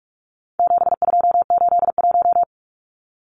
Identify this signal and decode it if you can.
Morse code, heard as a keyed tone, about 44 wpm, 700 Hz. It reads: 7281
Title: Numbers - 4-Digit 45wpm